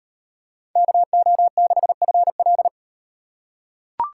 Morse code, heard as a keyed tone, about 36 wpm, 700 Hz. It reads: KO6FL